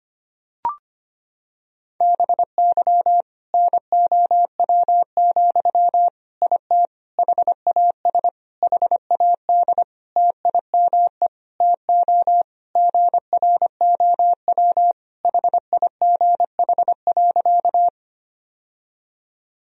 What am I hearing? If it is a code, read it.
Morse code, heard as a keyed tone, about 25 wpm, 700 Hz. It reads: BY NOW, IT HAS HAD TIME TO GROW HIGH.
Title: Sentences from Top 1000 Words - Review Part 1 - 25wpm